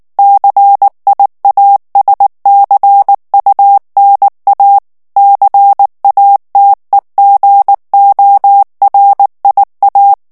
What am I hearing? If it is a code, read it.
Morse code, heard as a keyed tone, about 19 wpm, 800 Hz. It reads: CIASCUNA CATEGORIA
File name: morse mp3 / ITALIEN texte clair 2 vitesse 18 mots minute